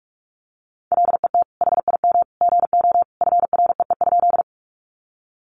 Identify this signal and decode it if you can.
Morse code, heard as a keyed tone, about 44 wpm, 700 Hz. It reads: LET HIM GO FREE?
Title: Sentences from Top 500 Words 45wpm